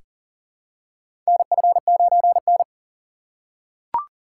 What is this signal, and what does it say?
Morse code, heard as a keyed tone, about 40 wpm, 700 Hz. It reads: NF9N
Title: USA Call Signs - 4 Characters - 2x Character Spacing - 40wpm